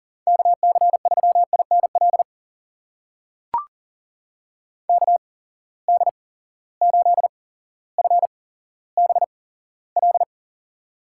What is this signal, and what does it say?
Morse code, heard as a keyed tone, about 39 wpm, spaced as for 17 wpm, 700 Hz. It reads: KC3INL KD8FBL
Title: USA Call Signs - 6 Characters - 8x Character Spacing - 40wpm